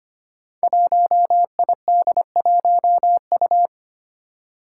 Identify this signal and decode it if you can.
Morse code, heard as a keyed tone, about 25 wpm, 700 Hz. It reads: 1ID1U
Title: Five Challenging Letter-Number Combinations 25wpm